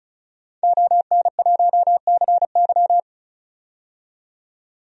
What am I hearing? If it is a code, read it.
Morse code, heard as a keyed tone, about 36 wpm, 700 Hz. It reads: ON1CY
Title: DXCC-100 Call Signs - 5 Characters - 35wpm